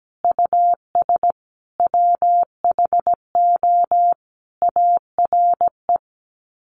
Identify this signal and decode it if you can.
Morse code, heard as a keyed tone, about 17 wpm, 700 Hz. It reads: US WHO ARE
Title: Sets of 5 Words Encore2 17wpm